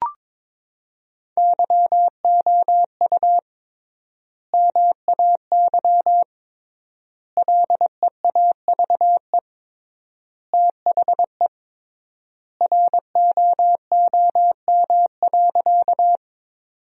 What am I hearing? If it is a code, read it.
Morse code, heard as a keyed tone, about 22 wpm, 700 Hz. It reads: YOU MAY LEAVE THE ROOM.
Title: 3x Word-Spacing - Sentences from Top 400 Words - 22wpm